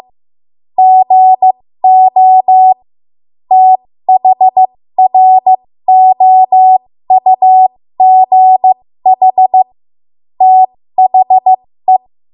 Morse code, heard as a keyed tone, about 15 wpm, spaced as for 13 wpm, 750 Hz. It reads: GO THROUGH THE
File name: Morse / 13WPM